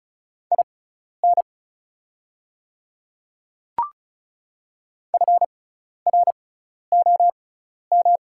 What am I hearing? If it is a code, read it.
Morse code, heard as a keyed tone, about 36 wpm, spaced as for 12 wpm, 700 Hz. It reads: IN FROM
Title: 6x Character Spacing - Top 100 Words - 35wpm